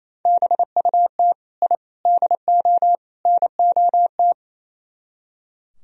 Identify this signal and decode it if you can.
Morse code, heard as a keyed tone, about 35 wpm, 700 Hz. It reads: BUT I DO NOT
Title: Speed-Racing - Sets of 4 Words 28wpm